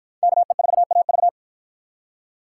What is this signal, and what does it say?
Morse code, heard as a keyed tone, about 47 wpm, 700 Hz. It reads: KE4AV